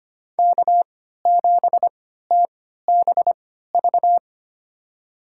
Morse code, heard as a keyed tone, about 25 wpm, spaced as for 14 wpm, 700 Hz. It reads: K7TBV
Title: USA Call Signs - 5 Characters - 3x Character Spacing - 25wpm